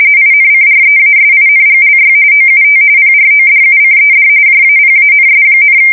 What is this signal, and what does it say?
RTTY, about 45.45 baud. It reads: INCREASINGLY OUT PRODUCED JAPANS,
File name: fsk170 / usnavy4